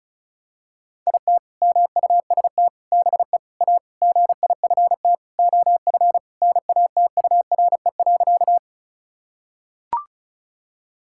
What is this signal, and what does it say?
Morse code, heard as a keyed tone, about 36 wpm, 700 Hz. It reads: IT MUST BE A GIFT OF NATURE.
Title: Sentences from Top 1000 Words - Review Part 6 - 35wpm